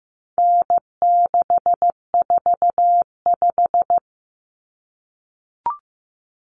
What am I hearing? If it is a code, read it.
Morse code, heard as a keyed tone, about 15 wpm, 700 Hz. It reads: N645
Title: Common Cut Numbers - 4-Digit 15wpm